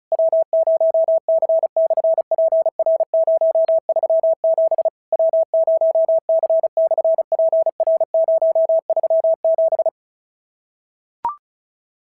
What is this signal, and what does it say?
Morse code, heard as a keyed tone, about 33 wpm, 650 Hz. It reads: W0C/PR037 W0C/PR037
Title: SOTA QSO Practice 35wpm